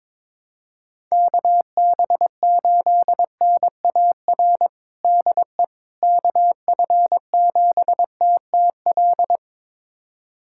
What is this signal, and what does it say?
Morse code, heard as a keyed tone, about 22 wpm, 700 Hz. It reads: KB8NAR DE KF7TTL
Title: USA Callsign DE Callsign - 6 Characters - 22wpm